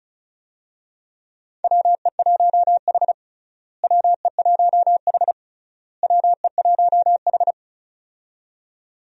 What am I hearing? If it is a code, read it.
Morse code, heard as a keyed tone, about 36 wpm, 700 Hz. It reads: WE1H WE1H WE1H
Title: USA Call Signs - 4 Characters - Repeated 3x - 35wpm